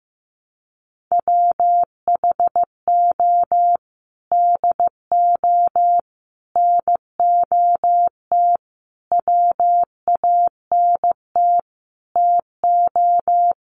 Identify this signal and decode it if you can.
Morse code, heard as a keyed tone, about 15 wpm, 700 Hz. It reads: WHO DO NOT WANT TO
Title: Sets of 5 Words Encore2 15wpm